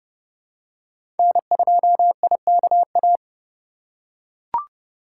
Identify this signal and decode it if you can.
Morse code, heard as a keyed tone, about 31 wpm, 700 Hz. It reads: N2IKA
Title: USA Call Signs - 5 Characters - 2x Character Spacing - 30wpm